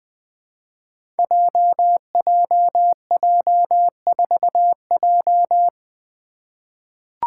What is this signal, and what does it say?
Morse code, heard as a keyed tone, about 20 wpm, 700 Hz. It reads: JJJ4J
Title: Five Challenging Letter-Number Combinations 20wpm